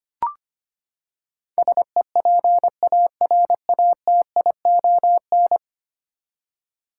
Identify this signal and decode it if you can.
Morse code, heard as a keyed tone, about 25 wpm, 700 Hz. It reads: SEPARATION
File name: Ten Letter Words - 25wpm